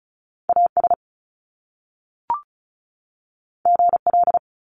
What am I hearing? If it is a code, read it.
Morse code, heard as a keyed tone, about 35 wpm, 700 Hz. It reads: AS GL
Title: Top Two Letter Combinations in Words 35wpm